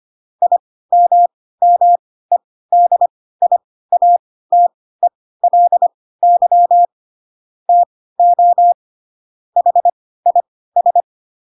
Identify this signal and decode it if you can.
Morse code, heard as a keyed tone, about 25 wpm, spaced as for 16 wpm, 700 Hz. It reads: IMMEDIATELY TO HIS
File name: Pinocchio / Pinocchio ch2 10 25 ik1hge